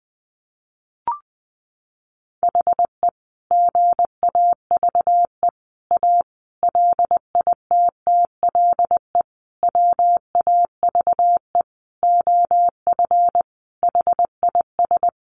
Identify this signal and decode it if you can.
Morse code, heard as a keyed tone, about 20 wpm, 700 Hz. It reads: HE GAVE A LITTLE WAVE OF HIS
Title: Sentences from Top 1400 Words 20wpm